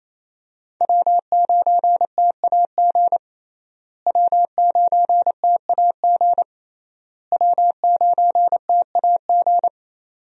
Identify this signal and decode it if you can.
Morse code, heard as a keyed tone, about 27 wpm, 700 Hz. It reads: W9TAG W9TAG W9TAG
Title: USA Call Signs - 5 Characters - Repeated 3x - 28wpm